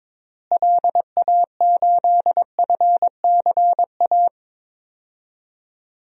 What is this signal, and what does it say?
Morse code, heard as a keyed tone, about 22 wpm, 700 Hz. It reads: LA8FCA
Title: DXCC-100 Call Signs - 6 Characters - Repeated 2x - 22wpm